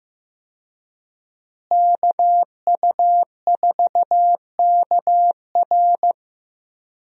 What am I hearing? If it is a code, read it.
Morse code, heard as a keyed tone, about 15 wpm, 700 Hz. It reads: KU4KR
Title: USA Call Signs - 5 Characters - Repeated 3x - 15wpm